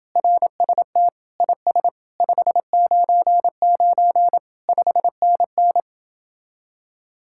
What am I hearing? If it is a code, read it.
Morse code, heard as a keyed tone, about 29 wpm, 700 Hz. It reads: RST IS 599 5NN